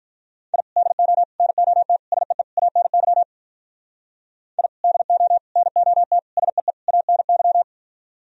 Morse code, heard as a keyed tone, about 44 wpm, 700 Hz. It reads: I DO NOT SEE ANY I DO NOT SEE ANY